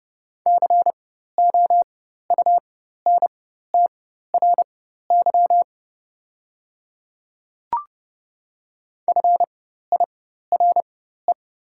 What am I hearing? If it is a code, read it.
Morse code, heard as a keyed tone, about 30 wpm, spaced as for 14 wpm, 700 Hz. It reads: COUNTRY FIRE
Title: 4x Character Spacing - Top 500 Words - 30wpm